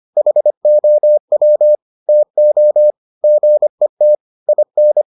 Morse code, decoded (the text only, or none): HOW TO GET IN